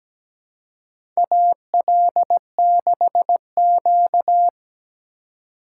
Morse code, heard as a keyed tone, about 17 wpm, 700 Hz. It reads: AL6Q
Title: USA Call Signs - 4 Characters - 3x Character Spacing - 17wpm